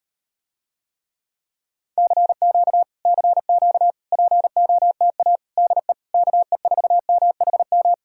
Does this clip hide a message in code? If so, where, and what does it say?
Yes: Morse code from 2.0 to 7.9 s, heard as a keyed tone, about 38 wpm, 700 Hz. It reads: CQ CQ POTA DE KE4MHM